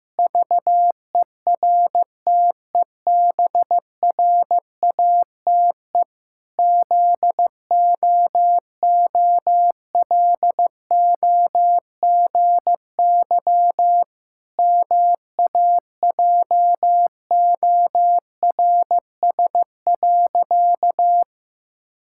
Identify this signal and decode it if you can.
Morse code, heard as a keyed tone, about 15 wpm, 700 Hz. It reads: VERTEBRATE ZOOLOGY MAJORS.
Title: Pangram Challenge v2 - 15wpm